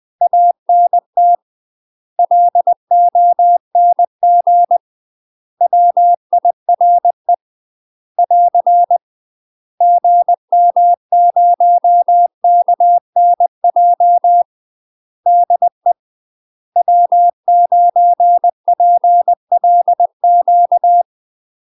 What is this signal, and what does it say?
Morse code, heard as a keyed tone, about 20 wpm, 700 Hz. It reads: ANT LONG WIRE <AR> GM0KNJ DE W9PLQ
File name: UK MASTER QSOs TRAINING / 20WPM / UK MASTER QSOs 0024